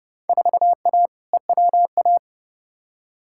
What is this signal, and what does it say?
Morse code, heard as a keyed tone, about 30 wpm, 700 Hz. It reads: 4A EWA
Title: Field Day Exchanges - 30wpm